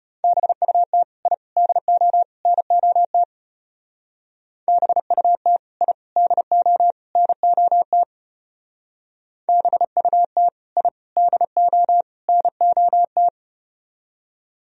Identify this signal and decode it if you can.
Morse code, heard as a keyed tone, about 33 wpm, 700 Hz. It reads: BUT I DO NOT BUT I DO NOT BUT I DO NOT